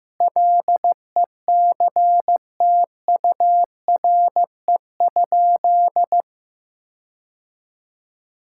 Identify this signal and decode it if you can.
Morse code, heard as a keyed tone, about 16 wpm, 700 Hz. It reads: LECTURE?
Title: Sentences from Top 1600 Words 15wpm